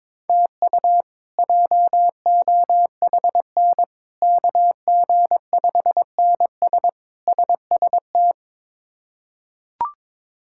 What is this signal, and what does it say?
Morse code, heard as a keyed tone, about 21 wpm, 700 Hz. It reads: TU JOHN KG5NS SST
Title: K1USN SST QSO Practice - 22wpm